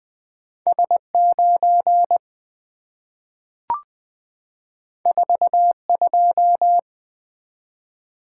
Morse code, heard as a keyed tone, about 20 wpm, 690 Hz. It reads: S9 42
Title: Two Letter-Number Combinations - 20wpm